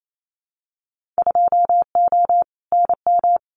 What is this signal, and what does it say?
Morse code, heard as a keyed tone, about 28 wpm, 700 Hz. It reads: 2O NM
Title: Winter Field Day Exchanges 28wpm